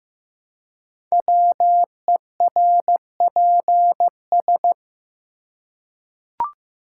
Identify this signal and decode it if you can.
Morse code, heard as a keyed tone, about 15 wpm, 700 Hz. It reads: WERPS
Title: Code Groups - 5 Letters - 15wpm